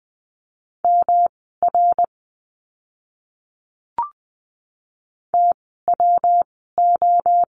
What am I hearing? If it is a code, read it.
Morse code, heard as a keyed tone, about 20 wpm, spaced as for 15 wpm, 700 Hz. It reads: MR TWO
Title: 244 Lesson 39f - Letter z - Words 20wpm